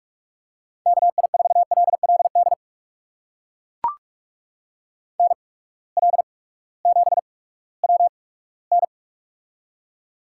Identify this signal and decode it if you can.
Morse code, heard as a keyed tone, about 44 wpm, spaced as for 20 wpm, 700 Hz. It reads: KI4LLD NL7WN
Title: USA Call Signs - 8x Character Spacing - 45wpm